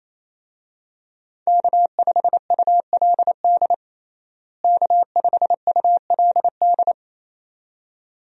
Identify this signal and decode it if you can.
Morse code, heard as a keyed tone, about 28 wpm, 700 Hz. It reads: K5ULD K5ULD
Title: USA Call Signs - 5 Characters - Repeated 2x - 28wpm